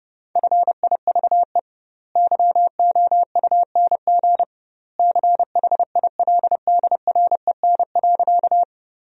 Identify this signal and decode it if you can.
Morse code, heard as a keyed tone, about 30 wpm, 700 Hz. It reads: FIVE YOUNG CHILDREN.
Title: 2x Word-Spacing - Sentences from Top 400 Words - 30wpm